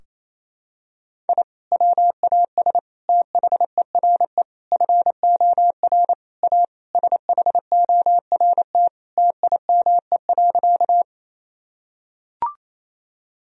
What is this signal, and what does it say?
Morse code, heard as a keyed tone, about 28 wpm, 700 Hz. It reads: I WAS THERE FOR A SHORT TIME.
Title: Sentences from Top 1000 Words - Review Part 2 - 28wpm